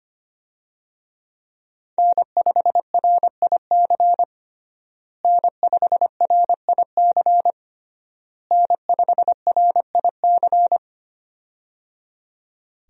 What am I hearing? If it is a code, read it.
Morse code, heard as a keyed tone, about 25 wpm, 700 Hz. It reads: N5RIC N5RIC N5RIC
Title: USA Call Signs - 5 Characters - Repeated 3x - 25wpm